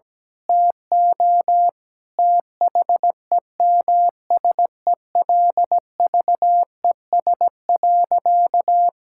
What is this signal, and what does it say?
Morse code, heard as a keyed tone, about 17 wpm, 700 Hz. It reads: TO THEMSELVES.
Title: Sentences from Top 1000 Words - Review Part 6 - 17wpm